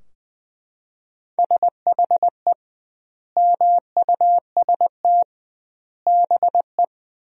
Morse code, heard as a keyed tone, about 20 wpm, 700 Hz. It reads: SHE MUST BE
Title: 2x Word-Spacing - Sentences from Top 400 Words - 20wpm